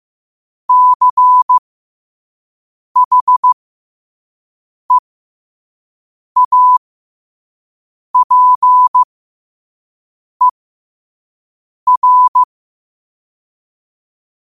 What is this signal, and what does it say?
Morse code, heard as a keyed tone, about 15 wpm, spaced as for 5 wpm, 1000 Hz. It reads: CHEAPER
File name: qotdfiles / 5wpm